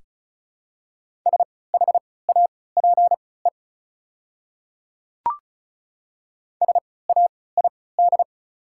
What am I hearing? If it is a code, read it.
Morse code, heard as a keyed tone, about 34 wpm, spaced as for 20 wpm, 690 Hz. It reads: SHAPE SAID
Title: 3x Character Spacing - Top 500 Words - 35wpm